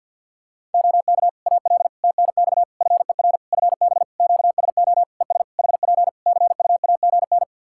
Morse code, heard as a keyed tone, about 52 wpm, 700 Hz. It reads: OK AL TNX FER FB QSO ES HP CUAGN
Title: Practice QSO - Exchange 4 - Ending - 50wpm